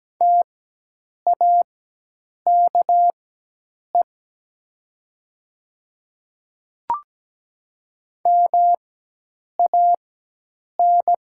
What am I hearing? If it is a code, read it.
Morse code, heard as a keyed tone, about 17 wpm, spaced as for 8 wpm, 700 Hz. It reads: TAKE MAN
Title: 4x Character Spacing - Top 500 Words - 17wpm